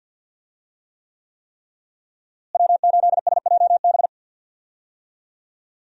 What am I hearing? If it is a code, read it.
Morse code, heard as a keyed tone, about 50 wpm, 700 Hz. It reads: W8SJB
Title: USA Call Signs - 5 Characters - Repeated 3x - 50wpm